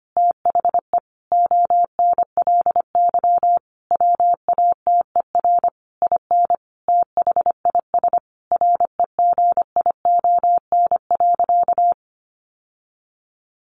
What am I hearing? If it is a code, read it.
Morse code, heard as a keyed tone, about 25 wpm, 700 Hz. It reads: THE ONLY WATER IN THIS REGION.
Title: Sentences from Top 1500 Words 25wpm